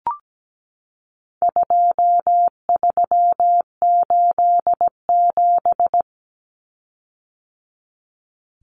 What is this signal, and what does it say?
Morse code, heard as a keyed tone, about 17 wpm, 700 Hz. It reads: 2387
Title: Common Cut Numbers - 4-Digit 17wpm